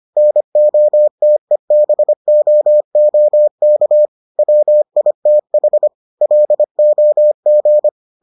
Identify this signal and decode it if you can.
Morse code, heard as a keyed tone, about 25 wpm, 600 Hz. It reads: NOTEBOOK WITH LOG